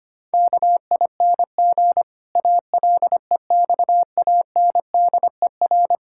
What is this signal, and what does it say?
Morse code, heard as a keyed tone, about 25 wpm, 700 Hz. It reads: KING ALEXANDER
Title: Pangram Challenge v2 - 25wpm